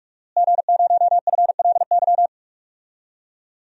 Morse code, heard as a keyed tone, about 46 wpm, 700 Hz. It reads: G0FLY
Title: DXCC-100 Call Signs - 5 Characters - Repeated 3x - 45wpm